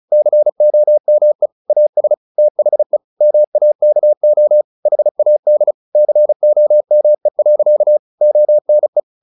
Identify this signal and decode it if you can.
Morse code, heard as a keyed tone, about 35 wpm, 600 Hz. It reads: COME AS THE MAKO HAD COME. ONE
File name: the old man and the sea 2 35wpm